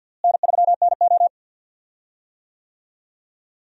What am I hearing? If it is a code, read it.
Morse code, heard as a keyed tone, about 47 wpm, 700 Hz. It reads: N3NO
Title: USA Call Signs - 4 Characters - Repeated 2x - 50wpm